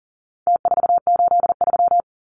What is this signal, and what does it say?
Morse code, heard as a keyed tone, about 40 wpm, 700 Hz. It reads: T483
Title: Numbers - Contest Style - 4-Digit 40wpm